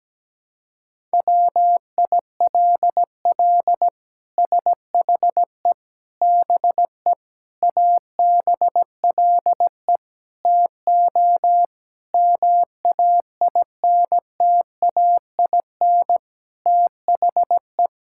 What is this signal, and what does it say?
Morse code, heard as a keyed tone, about 17 wpm, 700 Hz. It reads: WILL SHE BE ABLE TO MAINTAIN THE
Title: Sentences from Top 1000 Words - Review Part 5 - 17wpm